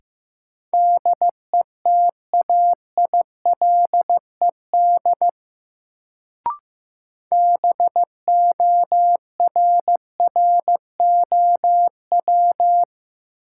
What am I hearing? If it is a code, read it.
Morse code, heard as a keyed tone, about 15 wpm, 700 Hz. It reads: DETAILED BORROW
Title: Sentences from Top 1500 Words - New Word Drill - 15wpm